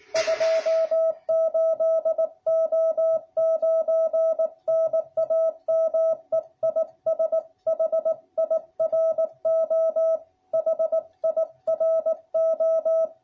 Morse code, heard as a keyed tone, about 18 wpm, 660 Hz. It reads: 28O9 NAME IS HIRO HIRO